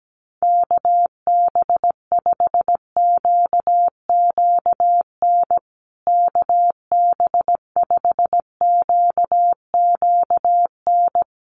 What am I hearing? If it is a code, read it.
Morse code, heard as a keyed tone, about 17 wpm, 700 Hz. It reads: KB5QQN KB5QQN
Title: USA Call Signs Repeated 3x 17wpm